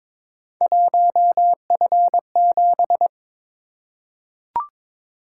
Morse code, heard as a keyed tone, about 22 wpm, 700 Hz. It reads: 1F7
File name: practice / 22wpm / Three Letter-Number Combinations - 22wpm